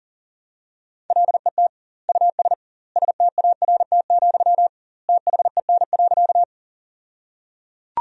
Morse code, heard as a keyed tone, about 37 wpm, 700 Hz. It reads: LET US START, THEN.
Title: 2x Word-Spacing - Sentences from Top 300 Words - 40wpm